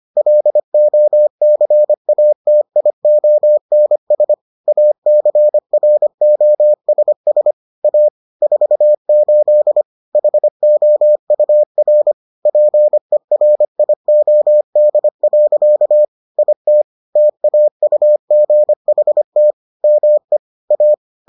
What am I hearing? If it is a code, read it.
Morse code, heard as a keyed tone, about 25 wpm, 600 Hz. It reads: LOCATIONS ACROSS A 48 HOUR PERIOD. IT TAUGHT ME A